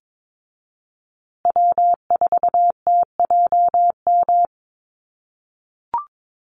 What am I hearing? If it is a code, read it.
Morse code, heard as a keyed tone, about 22 wpm, 700 Hz. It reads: W4TJM
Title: USA Call Signs Repeated 2x 22wpm